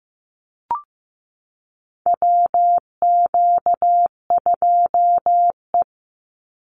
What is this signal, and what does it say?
Morse code, heard as a keyed tone, about 15 wpm, 700 Hz. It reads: WQ2E
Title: Call Signs Repeated 2x - 15wpm